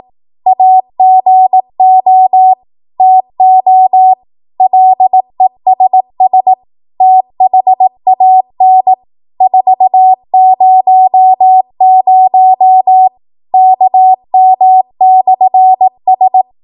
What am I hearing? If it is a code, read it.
Morse code, heard as a keyed tone, about 18 wpm, 750 Hz. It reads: AGO TO LESS THAN 400 KM/S
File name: Morse / 18WPM